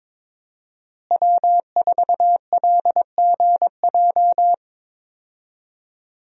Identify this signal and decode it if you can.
Morse code, heard as a keyed tone, about 22 wpm, 700 Hz. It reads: W4LGJ
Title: USA Call Signs - 5 Characters - Repeated 2x - 22wpm